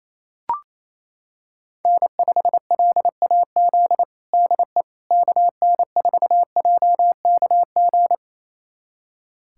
Morse code, heard as a keyed tone, about 28 wpm, 700 Hz. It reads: N5LAZ DE KN4JKG
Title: USA Callsign DE Callsign 28wpm